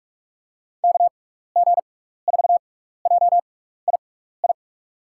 Morse code, heard as a keyed tone, about 42 wpm, spaced as for 15 wpm, 700 Hz. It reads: KG4JII
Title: USA Call Signs - 6x Character Spacing - 45wpm